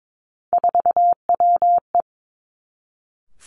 Morse code, heard as a keyed tone, about 22 wpm, 700 Hz. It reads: 4WE
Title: Three Challenging Letter-Number Combinations 22wpm